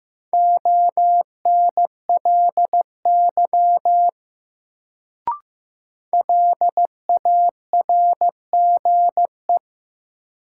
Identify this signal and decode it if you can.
Morse code, heard as a keyed tone, about 15 wpm, 700 Hz. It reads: ONLY LARGE
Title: Sentences from Top 200 Words - New Word Drill - 15wpm